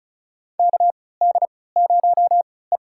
Morse code, heard as a keyed tone, about 36 wpm, spaced as for 20 wpm, 700 Hz. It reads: KD0E
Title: USA Call Signs - 4 Characters - 3x Character Spacing - 35wpm